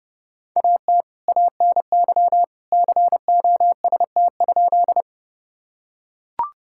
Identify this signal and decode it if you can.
Morse code, heard as a keyed tone, about 31 wpm, 700 Hz. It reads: AT ANY COST?
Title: Sentences from Top 1000 Words - Review Part 5 - 30wpm